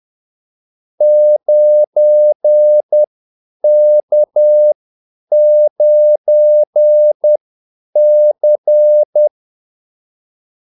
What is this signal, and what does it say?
Morse code, heard as a keyed tone, about 10 wpm, spaced as for 8 wpm, 600 Hz. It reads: 9K9C
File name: Callsigns-10-8wpm